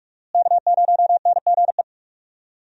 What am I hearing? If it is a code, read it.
Morse code, heard as a keyed tone, about 46 wpm, 700 Hz. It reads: K0NGE
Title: USA Call Signs - 5 Characters - Repeated 2x - 45wpm